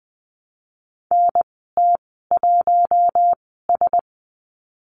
Morse code, heard as a keyed tone, about 20 wpm, spaced as for 15 wpm, 700 Hz. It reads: NT1S
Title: 048 Lesson 9g - Number 4 - Call Signs 20wpm